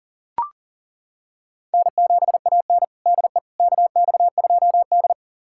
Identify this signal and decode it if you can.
Morse code, heard as a keyed tone, about 37 wpm, 700 Hz. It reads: N7AN DE KX2D
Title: USA Callsign DE Callsign - 4 Characters - 40wpm